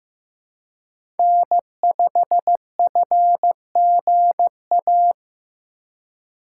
Morse code, heard as a keyed tone, about 15 wpm, 700 Hz. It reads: N5FGA
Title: USA Call Signs - 5 Characters - 4x Character Spacing - 15wpm